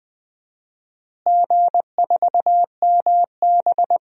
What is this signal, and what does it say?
Morse code, heard as a keyed tone, about 20 wpm, 700 Hz. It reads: G4MB
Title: DXCC-100 Call Signs - 4 Characters - Repeated 3x - 20wpm